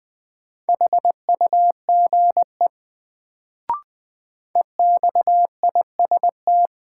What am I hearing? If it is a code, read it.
Morse code, heard as a keyed tone, about 20 wpm, 700 Hz. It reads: HUGE EXIST
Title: Sentences from Top 1100 Words - New Word Drill Alt - 20wpm